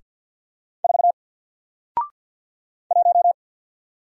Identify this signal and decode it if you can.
Morse code, heard as a keyed tone, about 54 wpm, 700 Hz. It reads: 4 1
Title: Single Letter-Number Alt 50wpm